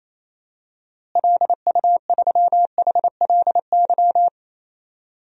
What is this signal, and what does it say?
Morse code, heard as a keyed tone, about 28 wpm, 700 Hz. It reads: LU3HLY
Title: DXCC-100 Call Signs - 6 Characters - Repeated 3x - 28wpm